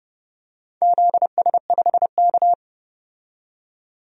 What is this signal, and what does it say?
Morse code, heard as a keyed tone, about 31 wpm, 700 Hz. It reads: ZS5K
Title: DXCC-100 Call Signs - 4 Characters - 30wpm